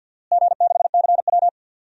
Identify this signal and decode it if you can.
Morse code, heard as a keyed tone, about 49 wpm, 700 Hz. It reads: G6CW
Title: DXCC-100 Call Signs - 4 Characters - Repeated 3x - 50wpm